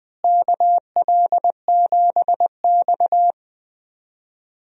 Morse code, heard as a keyed tone, about 20 wpm, 700 Hz. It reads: KL7X
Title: DXCC-100 Call Signs - 4 Characters - Repeated 2x - 20wpm